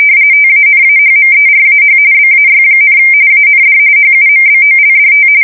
RTTY, about 45.45 baud. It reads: TANAHMERAH BAY, AND HUMBOLDT
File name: fsk170 / USS-New-Jersey